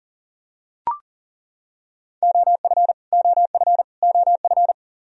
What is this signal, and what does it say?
Morse code, heard as a keyed tone, about 40 wpm, 700 Hz. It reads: OF OF OF
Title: Top 100 Words Repeated 40wpm